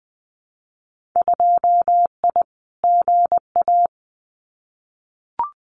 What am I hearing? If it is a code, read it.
Morse code, heard as a keyed tone, about 19 wpm, 700 Hz. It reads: 2I GA
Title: Winter Field Day Exchanges 20wpm